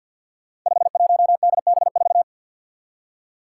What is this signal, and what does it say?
Morse code, heard as a keyed tone, about 47 wpm, 700 Hz. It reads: 51DB4
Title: Five Challenging Letter-Number Combinations 50wpm